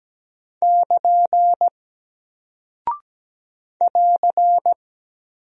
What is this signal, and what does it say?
Morse code, heard as a keyed tone, about 17 wpm, 700 Hz. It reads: <KN> <AR>
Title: Single Character (Letter:Number:Punctuation:Prosign) 17wpm